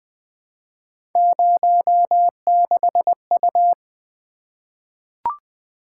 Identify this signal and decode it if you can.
Morse code, heard as a keyed tone, about 20 wpm, 700 Hz. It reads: 06U
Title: Three Letter-Number Combinations - 20wpm